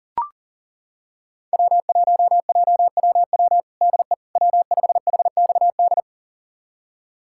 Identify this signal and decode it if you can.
Morse code, heard as a keyed tone, about 40 wpm, 700 Hz. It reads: W1JWW DE W5HXD
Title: USA Callsign DE Callsign 40wpm